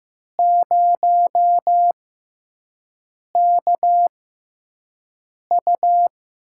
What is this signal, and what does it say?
Morse code, heard as a keyed tone, about 15 wpm, 700 Hz. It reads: 0 K U
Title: USA Call Signs - 6x Character Spacing - 15wpm